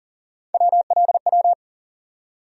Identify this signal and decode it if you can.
Morse code, heard as a keyed tone, about 40 wpm, 700 Hz. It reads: WLW
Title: USA Call Sign Postfixes - 3 Letters 40wpm